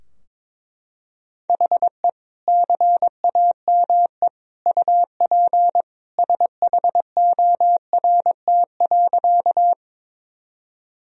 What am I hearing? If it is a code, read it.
Morse code, heard as a keyed tone, about 22 wpm, 700 Hz. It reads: HE CAME UP SHORT.
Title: Sentences from Top 1000 Words - Review Part 2 - 22wpm